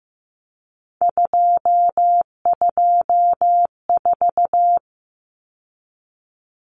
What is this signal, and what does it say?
Morse code, heard as a keyed tone, about 15 wpm, 700 Hz. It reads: 224
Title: Common Cut Numbers - 3-Digit 15wpm